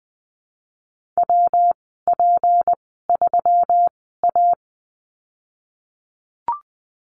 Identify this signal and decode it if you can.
Morse code, heard as a keyed tone, about 20 wpm, spaced as for 15 wpm, 700 Hz. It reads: WP3A
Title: 219 Lesson 35g - Number 0 - Call Signs 20wpm